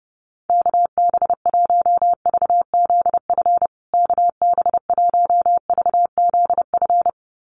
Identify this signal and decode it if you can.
Morse code, heard as a keyed tone, about 30 wpm, 700 Hz. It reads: KB1VZF KB1VZF
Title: USA Call Signs Repeated 2x 30wpm